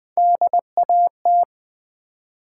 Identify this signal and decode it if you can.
Morse code, heard as a keyed tone, about 20 wpm, 700 Hz. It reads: DAT
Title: Build A Word - 1 Letter At A Time - 4-Letter Words - 20wpm